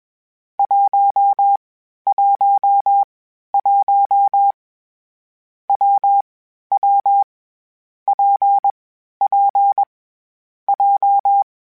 Morse code, heard as a keyed tone, about 21 wpm, 800 Hz. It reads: 1 1 1 W W P P J
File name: Variable Speed / Variable Speed-mp3s / WPJ WPJ1 sp @20